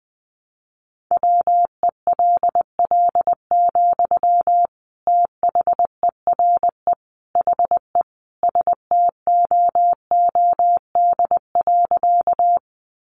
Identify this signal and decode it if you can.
Morse code, heard as a keyed tone, about 20 wpm, 700 Hz. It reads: WELL, THERE HE STOOD.
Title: Sentences from Top 500 Words 20wpm